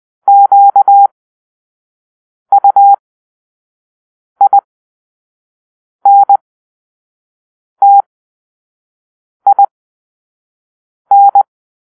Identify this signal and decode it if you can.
Morse code, heard as a keyed tone, about 20 wpm, spaced as for 5 wpm, 800 Hz. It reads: QUINTIN
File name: arrl/sound/Grp-7 Drill-4